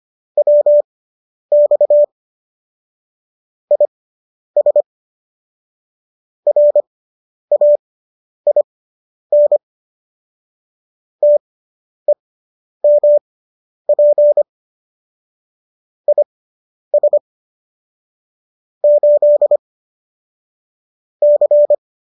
Morse code, heard as a keyed tone, about 25 wpm, spaced as for 10 wpm, 590 Hz. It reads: WX IS RAIN TEMP IS 8 C